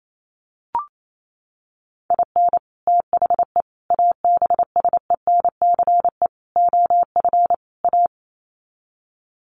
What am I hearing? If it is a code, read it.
Morse code, heard as a keyed tone, about 28 wpm, 700 Hz. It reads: IN THE ABSENCE OF A
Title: Sets of 5 Words Encore 28wpm